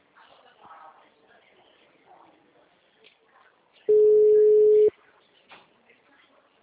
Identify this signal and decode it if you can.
Ringback tone (Europe, 425 Hz).
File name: out- -2008